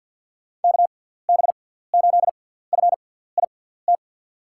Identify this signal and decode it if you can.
Morse code, heard as a keyed tone, about 47 wpm, spaced as for 17 wpm, 700 Hz. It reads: KB8FIT
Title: USA Call Signs - 6x Character Spacing - 50wpm